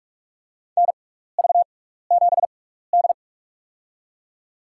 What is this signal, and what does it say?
Morse code, heard as a keyed tone, about 44 wpm, spaced as for 15 wpm, 700 Hz. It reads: NV7D
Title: USA Call Signs - 4 Characters - 6x Character Spacing - 45wpm